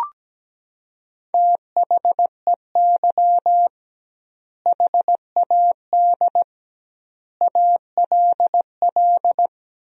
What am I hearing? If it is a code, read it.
Morse code, heard as a keyed tone, about 17 wpm, 700 Hz. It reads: THEY HAD ALL
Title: 2x Word-Spacing - Sentences from Top 200 Words - 17wpm